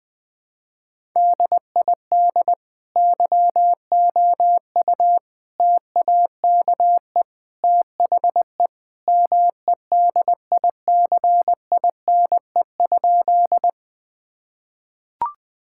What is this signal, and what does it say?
Morse code, heard as a keyed tone, about 21 wpm, 700 Hz. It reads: DID YOU TAKE THE MEDICINE?
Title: Sentences from Top 1000 Words - Review Part 6 - 20wpm